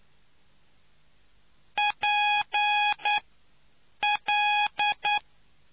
Morse code, heard as a keyed tone, about 9 wpm, 830 Hz. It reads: P L